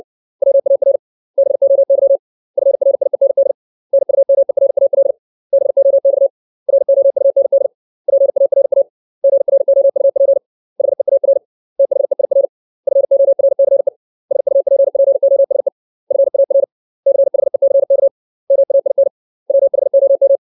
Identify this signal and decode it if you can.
Morse code, heard as a keyed tone, about 60 wpm, 550 Hz. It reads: WAR BOY FRIEND NUMERAL BOX ROUND PARA GROUP HEAR THEIR FORCE SUPPOSE FAR CHECK MAIN WHOM
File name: 632 words / 632 Words 60WPM